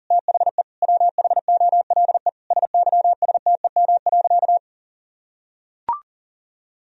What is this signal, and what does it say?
Morse code, heard as a keyed tone, about 40 wpm, 700 Hz. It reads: THE WHOLE SYSTEM.
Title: Sentences from Top 1000 Words - Review Part 3 - 40wpm